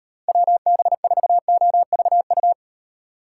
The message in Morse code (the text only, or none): WB4OVU